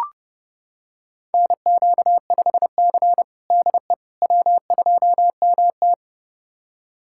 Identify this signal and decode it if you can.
Morse code, heard as a keyed tone, about 30 wpm, 700 Hz. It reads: NQ5C DE W2MT
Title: USA Callsign DE Callsign - 4 Characters - 30wpm